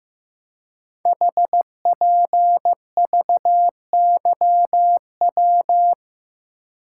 Morse code, heard as a keyed tone, about 15 wpm, 700 Hz. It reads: HPVYW
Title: Code Groups - 5 Letters - 15wpm